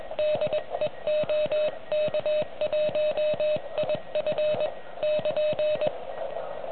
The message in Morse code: DE OK1IF <KN>